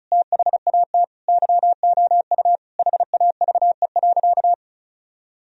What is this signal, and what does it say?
Morse code, heard as a keyed tone, about 36 wpm, 700 Hz. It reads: THAT YOU HAVE.
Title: Sentences from Top 1000 Words - Review Part 4 - 35wpm